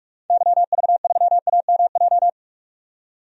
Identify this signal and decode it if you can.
Morse code, heard as a keyed tone, about 41 wpm, 700 Hz. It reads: YV3AMJ
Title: DXCC-100 Call Signs - 6 Characters - 45wpm